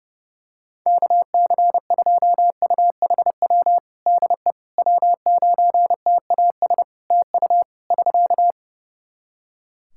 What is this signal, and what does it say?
Morse code, heard as a keyed tone, about 28 wpm, 700 Hz. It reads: KC2UHW DE W9TAS TU <SK>
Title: Practice QSO - Exchange 4 - Ending - 30wpm